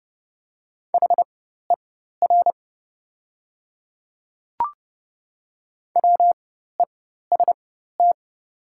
Morse code, heard as a keyed tone, about 29 wpm, spaced as for 14 wpm, 690 Hz. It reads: HER WEST
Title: 4x Character Spacing - Top 500 Words - 30wpm